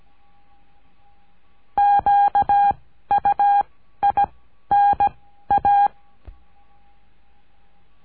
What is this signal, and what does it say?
Morse code, heard as a keyed tone, about 17 wpm, 810 Hz. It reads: Q U I N A
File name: cursos / CURSO DE TELEGRAFIA / Telegrafia - Trilha 14